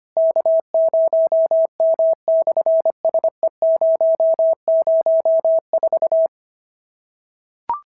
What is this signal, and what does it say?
Morse code, heard as a keyed tone, about 24 wpm, 650 Hz. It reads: K0M/SE004
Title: SOTA QSO Practice 25wpm